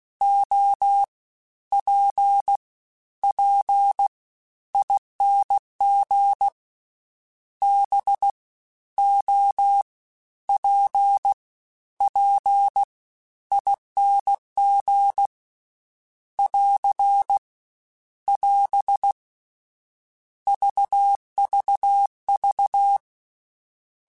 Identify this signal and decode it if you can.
Morse code, heard as a keyed tone, about 16 wpm, 800 Hz. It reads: O P P ING B O P P ING <AR> <AS> VVV
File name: Misc / Misc-mp3s / Jumps @15